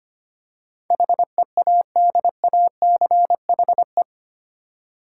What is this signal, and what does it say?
Morse code, heard as a keyed tone, about 25 wpm, 700 Hz. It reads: HEADACHE
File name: 8 Letter Words - First 3 Letters Given - 25wpm